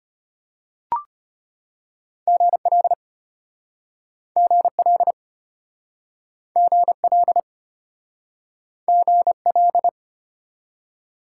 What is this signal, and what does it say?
Morse code, heard as a keyed tone, about 33 wpm, 700 Hz. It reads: GL GL GL GL